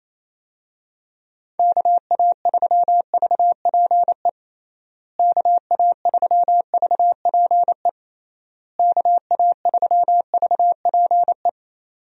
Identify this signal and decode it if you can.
Morse code, heard as a keyed tone, about 28 wpm, 700 Hz. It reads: KA3VPE KA3VPE KA3VPE
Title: USA Call Signs - 6 Characters - Repeated 3x - 28wpm